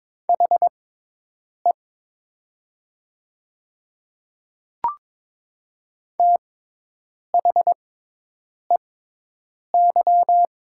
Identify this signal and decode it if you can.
Morse code, heard as a keyed tone, about 22 wpm, spaced as for 8 wpm, 700 Hz. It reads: HE THEY
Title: 6x Character Spacing - Top 200 Words - 22wpm